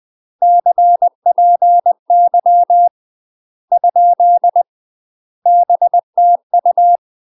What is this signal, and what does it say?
Morse code, heard as a keyed tone, about 20 wpm, 700 Hz. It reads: CPY ? BTU
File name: UK MASTER QSOs TRAINING / 20WPM / UK MASTER QSOs 0002